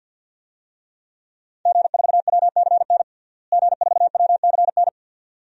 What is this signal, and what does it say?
Morse code, heard as a keyed tone, about 49 wpm, 700 Hz. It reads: G4WCN G4WCN
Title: DXCC-100 Call Signs - 5 Characters - Repeated 2x - 50wpm